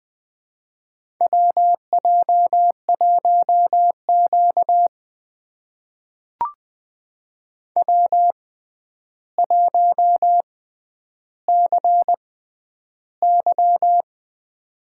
Morse code, heard as a keyed tone, about 20 wpm, spaced as for 10 wpm, 700 Hz. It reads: WJ1Q W1CY
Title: USA Call Signs - 4 Characters - 6x Character Spacing - 20wpm